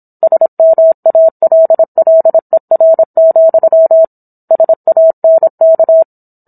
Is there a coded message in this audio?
Yes: Morse code from 0.2 to 6.0 s, heard as a keyed tone, about 27 wpm, 650 Hz. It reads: SMALLER, SANK